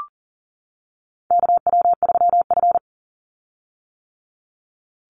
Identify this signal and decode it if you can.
Morse code, heard as a keyed tone, about 41 wpm, 700 Hz. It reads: KW3F
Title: Call Signs - 40wpm